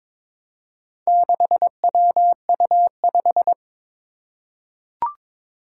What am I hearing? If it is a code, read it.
Morse code, heard as a keyed tone, about 22 wpm, 700 Hz. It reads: 6WU5
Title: Four Challenging Letter-Number Combinations 22wpm